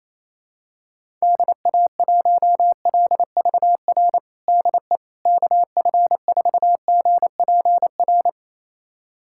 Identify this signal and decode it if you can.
Morse code, heard as a keyed tone, about 28 wpm, 700 Hz. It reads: DA1LVR DE KF4GPR
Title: USA Callsign DE Callsign - 6 Characters - 28wpm